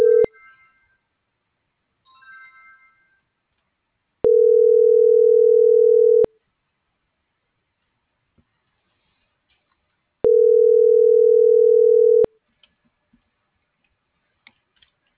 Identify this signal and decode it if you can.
Ringback tone (North America), 440 + 480 Hz.